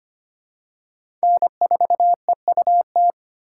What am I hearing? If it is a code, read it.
Morse code, heard as a keyed tone, about 25 wpm, 700 Hz. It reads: N4EUT
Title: USA Call Signs - 5 Characters - 4x Character Spacing - 25wpm